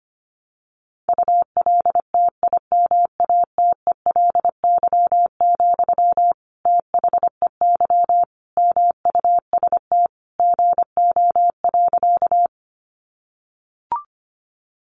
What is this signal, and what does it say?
Morse code, heard as a keyed tone, about 25 wpm, 700 Hz. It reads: ULTIMATELY, THEY MUST GO.
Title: Sentences from Top 1200 Words 25wpm